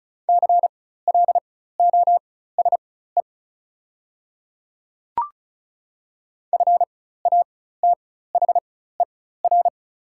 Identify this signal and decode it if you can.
Morse code, heard as a keyed tone, about 36 wpm, spaced as for 16 wpm, 690 Hz. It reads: CLOSE FATHER
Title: 4x Character Spacing - Top 300 Words - 35wpm